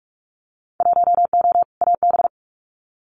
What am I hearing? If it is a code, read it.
Morse code, heard as a keyed tone, about 46 wpm, 690 Hz. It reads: 1O AB
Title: Winter Field Day Exchanges Encore 45wpm